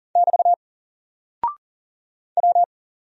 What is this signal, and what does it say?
Morse code, heard as a keyed tone, about 38 wpm, 700 Hz. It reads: <BT> W
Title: Single Character (Letter:Number:Punctuation:Prosign) Alt 40wpm